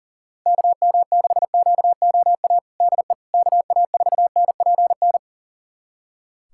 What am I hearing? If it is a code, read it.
Morse code, heard as a keyed tone, about 40 wpm, 700 Hz. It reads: KM6QOA DE KA4NPN
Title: USA Callsign DE Callsign - 6 Characters - 40wpm